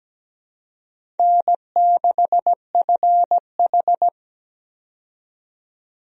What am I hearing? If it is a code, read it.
Morse code, heard as a keyed tone, about 17 wpm, 700 Hz. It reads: N6FH
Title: USA Call Signs - 4 Characters - 17wpm